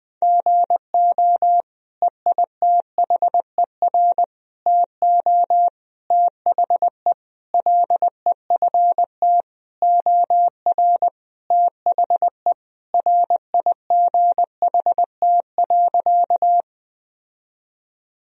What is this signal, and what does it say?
Morse code, heard as a keyed tone, about 21 wpm, 700 Hz. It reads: GO EITHER TO THE LEFT OR THE RIGHT.
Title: Sentences from Top 1000 Words - Review Part 4 - 20wpm